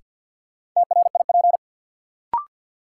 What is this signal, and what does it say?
Morse code, heard as a keyed tone, about 53 wpm, 700 Hz. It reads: TRIP
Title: Sentences from Top 1000 Words - Review Part 4 - New Word Drill - 50wpm